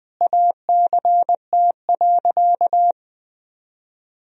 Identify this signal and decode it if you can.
Morse code, heard as a keyed tone, about 21 wpm, 700 Hz. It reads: ACT.
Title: Sentences from Top 1000 Words - Review Part 1 - 20wpm